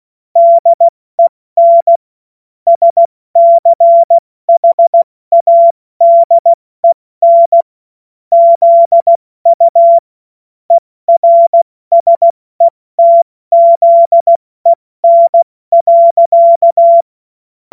Morse code, heard as a keyed tone, about 16 wpm, 680 Hz. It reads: DEN SCHADEN ZU ERSETZEN.
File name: Morsekurs / Witze / Witz 03 Juni Mai 2025 70 F80